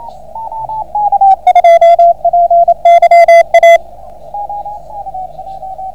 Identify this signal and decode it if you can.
Morse code, heard as a keyed tone, about 28 wpm, 690 Hz, with radio noise. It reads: JPYA